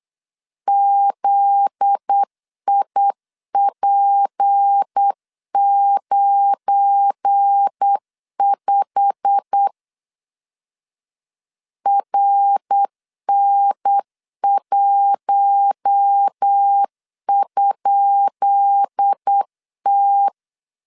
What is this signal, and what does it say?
Morse code, heard as a keyed tone, about 8 wpm, 800 Hz. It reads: ZIP95 RN1?T
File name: LEZIONI / 10 SIGNS / 10B INT 8wpm 3534